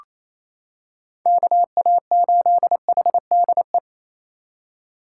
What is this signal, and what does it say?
Morse code, heard as a keyed tone, about 28 wpm, 700 Hz. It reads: KA8HDE
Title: K1USN SST QSO Practice - 28wpm